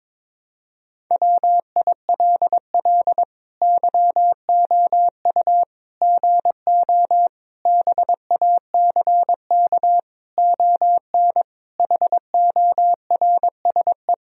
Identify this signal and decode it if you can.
Morse code, heard as a keyed tone, about 22 wpm, 700 Hz. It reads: WILL YOU GO BACK ON HORSE
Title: Sentences from Top 1000 Words - Review Part 3 - 22wpm